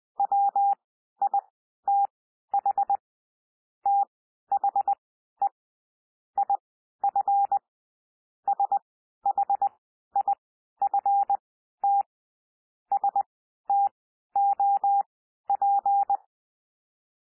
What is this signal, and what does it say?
Morse code, heard as a keyed tone, about 20 wpm, spaced as for 12 wpm, 800 Hz. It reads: WITH THE IF SHIFT STOP
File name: cwmp3 / P1 SOUND PASSAGE 20 wpm